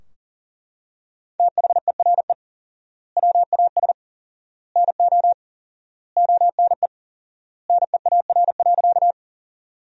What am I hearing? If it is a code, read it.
Morse code, heard as a keyed tone, about 37 wpm, 700 Hz. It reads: THERE WAS NO ONE NEAR.
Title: 4x Word-Spacing - Sentences from Top 200 Words - 40wpm